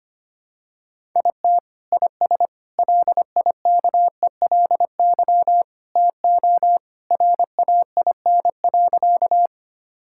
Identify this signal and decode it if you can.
Morse code, heard as a keyed tone, about 25 wpm, 700 Hz. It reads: IT IS LIKELY TO RAIN.
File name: practice / 25wpm / Sentences with 2-Word Phrases - 25wpm